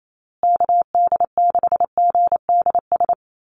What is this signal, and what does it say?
Morse code, heard as a keyed tone, about 28 wpm, 700 Hz. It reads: KD6GDS
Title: USA Call Signs Repeated 3x 28wpm